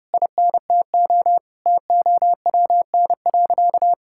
Morse code, heard as a keyed tone, about 30 wpm, 700 Hz. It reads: INTO TOWN.
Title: Sentences from Top 1000 Words - Review Part 6 - 30wpm